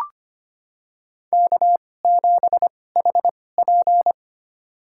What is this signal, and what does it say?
Morse code, heard as a keyed tone, about 25 wpm, spaced as for 18 wpm, 700 Hz. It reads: K7HP
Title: USA Call Signs - 4 Characters - 2x Character Spacing - 25wpm